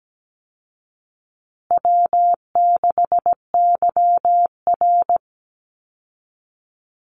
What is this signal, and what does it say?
Morse code, heard as a keyed tone, about 17 wpm, 700 Hz. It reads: W6YR
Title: Easy Call Signs Repeated 3x - 17wpm